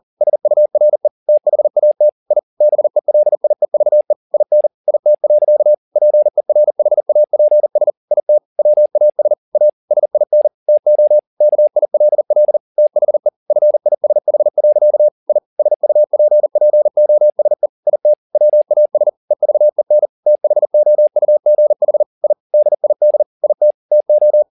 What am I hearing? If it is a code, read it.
Morse code, heard as a keyed tone, about 40 wpm, 600 Hz. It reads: SURE THAT I BELIEVE IN IT. PERHAPS IT WAS A SIN TO KILL THE FISH. I SUPPOSE IT WAS EVEN THOUGH I DID IT TO
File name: the old man and the sea 2 40wpm